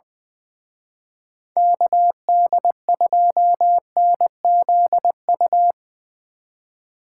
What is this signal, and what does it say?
Morse code, heard as a keyed tone, about 23 wpm, 700 Hz. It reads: KD2NZU
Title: Speed Racing - USA 2x3 Call Signs 20wpm